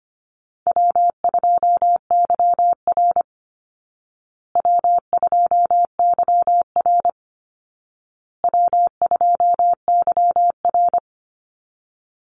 Morse code, heard as a keyed tone, about 25 wpm, 700 Hz. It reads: W2YR W2YR W2YR
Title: Call Signs Repeated 2x - 25wpm